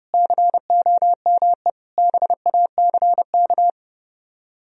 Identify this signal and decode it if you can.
Morse code, heard as a keyed tone, about 38 wpm, 700 Hz. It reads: COME BACK